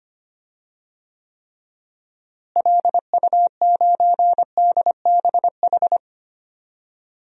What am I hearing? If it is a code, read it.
Morse code, heard as a keyed tone, about 25 wpm, 700 Hz. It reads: LU9DBH
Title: DXCC-100 Call Signs - 6 Characters - Repeated 2x - 25wpm